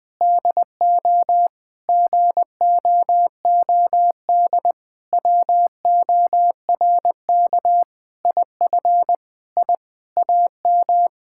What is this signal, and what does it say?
Morse code, heard as a keyed tone, about 20 wpm, 700 Hz. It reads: DO GOOD WORK IF I AM
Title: Sentences from Top 1000 Words - Review Part 6 - 20wpm